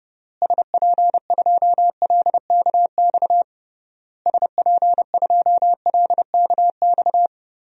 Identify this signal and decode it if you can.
Morse code, heard as a keyed tone, about 30 wpm, 700 Hz. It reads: SP2LKX SP2LKX
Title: DXCC-100 Call Signs - 6 Characters - Repeated 3x - 30wpm